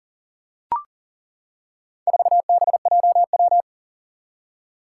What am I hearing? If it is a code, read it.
Morse code, heard as a keyed tone, about 40 wpm, 700 Hz. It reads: 4BJW
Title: Four Challenging Letter-Number Combinations 40wpm